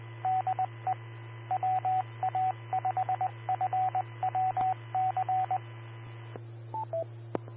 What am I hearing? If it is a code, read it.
Morse code, heard as a keyed tone, about 20 wpm, 750 Hz, with radio noise. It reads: DE WA5FWC
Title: Austin, TX Amateur Repeaters